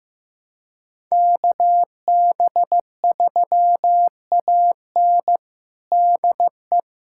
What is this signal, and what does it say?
Morse code, heard as a keyed tone, about 15 wpm, 700 Hz. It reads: KB3AN DE
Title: USA Callsign DE Callsign - 5 Characters - 15wpm